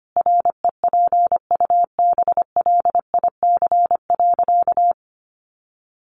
Morse code, heard as a keyed tone, about 25 wpm, 700 Hz. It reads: REPUBLIC.
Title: Sentences from Top 1500 Words 25wpm